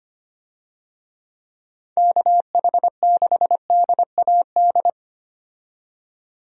Morse code, heard as a keyed tone, about 25 wpm, 700 Hz. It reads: KH6DAD
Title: DXCC-100 Call Signs - 6 Characters - 25wpm